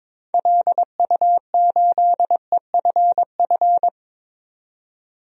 Morse code, heard as a keyed tone, about 22 wpm, 700 Hz. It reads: LU8EFF
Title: DXCC-100 Call Signs - 6 Characters - 22wpm